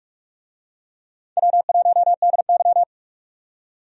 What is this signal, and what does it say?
Morse code, heard as a keyed tone, about 46 wpm, 700 Hz. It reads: W1DY
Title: USA Call Signs - 4 Characters - 45wpm